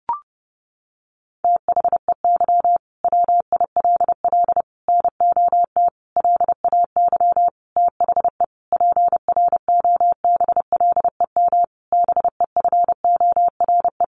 Morse code, heard as a keyed tone, about 30 wpm, 700 Hz. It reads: THEY WILL NOT LAY THE PROBLEM BEFORE
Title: Sentences from Top 1500 Words 30wpm